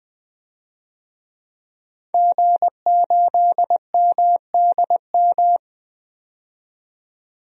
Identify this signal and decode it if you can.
Morse code, heard as a keyed tone, about 20 wpm, 700 Hz. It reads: G8MDM
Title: DXCC-100 Call Signs - 5 Characters - 20wpm